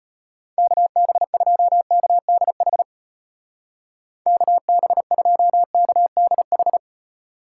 Morse code, heard as a keyed tone, about 32 wpm, 700 Hz. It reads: KB2KDH KB2KDH